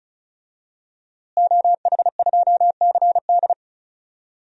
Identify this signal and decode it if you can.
Morse code, heard as a keyed tone, about 36 wpm, 700 Hz. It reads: OH2CD
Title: DXCC-100 Call Signs - 5 Characters - Repeated 2x - 35wpm